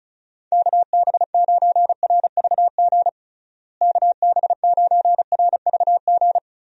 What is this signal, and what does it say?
Morse code, heard as a keyed tone, about 36 wpm, 700 Hz. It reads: KB9RVG KB9RVG
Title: USA Call Signs - 6 Characters - Repeated 2x - 35wpm